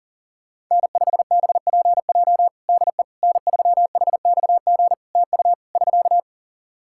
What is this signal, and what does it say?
Morse code, heard as a keyed tone, about 40 wpm, 700 Hz. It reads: N5BPJ DE N3HXG TU <SK>
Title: K1USN SST QSO Practice 40wpm